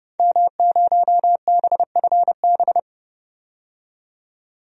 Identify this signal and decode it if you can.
Morse code, heard as a keyed tone, about 31 wpm, 700 Hz. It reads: M0BFB
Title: DXCC-100 Call Signs - 5 Characters - 30wpm